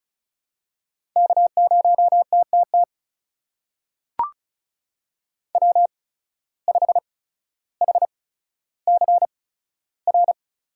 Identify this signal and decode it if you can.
Morse code, heard as a keyed tone, about 36 wpm, spaced as for 15 wpm, 700 Hz. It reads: K0TTT W5HCR
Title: USA Call Signs - 5 Characters - 8x Character Spacing - 35wpm